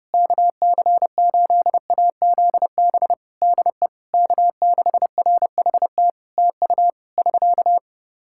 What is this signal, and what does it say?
Morse code, heard as a keyed tone, about 30 wpm, 700 Hz. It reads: KC8AZB DE K6RHT TU <SK>
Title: Practice QSO - Exchange 4 - Ending - 30wpm